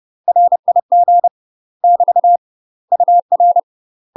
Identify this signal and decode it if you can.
Morse code, heard as a keyed tone, about 30 wpm, 700 Hz. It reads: RIG <BT> UR
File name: UK MASTER QSOs TRAINING / 30WPM / UK MASTER QSOs 0038